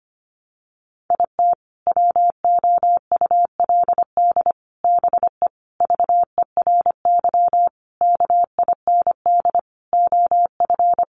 Morse code, heard as a keyed tone, about 25 wpm, 700 Hz. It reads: IT WOULD BE VERY KIND OF
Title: Sentences from Top 200 Words 25wpm